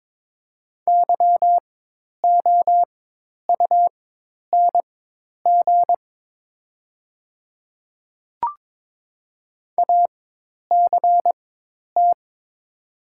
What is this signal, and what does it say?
Morse code, heard as a keyed tone, about 22 wpm, spaced as for 10 wpm, 700 Hz. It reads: YOUNG ACT
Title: 4x Character Spacing - Top 500 Words - 22wpm